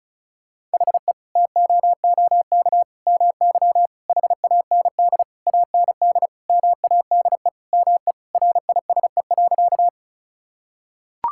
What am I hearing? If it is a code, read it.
Morse code, heard as a keyed tone, about 36 wpm, 700 Hz. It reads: HE TOOK MY HAND AND MADE ME RISE.
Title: Sentences from Top 1000 Words - Review Part 6 - 35wpm